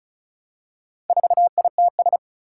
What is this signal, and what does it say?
Morse code, heard as a keyed tone, about 36 wpm, 700 Hz. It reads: 4ITS
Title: Four Challenging Letter-Number Combinations 35wpm